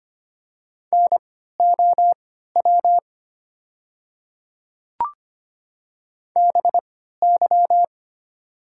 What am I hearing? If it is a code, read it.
Morse code, heard as a keyed tone, about 25 wpm, spaced as for 14 wpm, 690 Hz. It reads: NOW BY
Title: 3x Character Spacing - Top 100 Words - 25wpm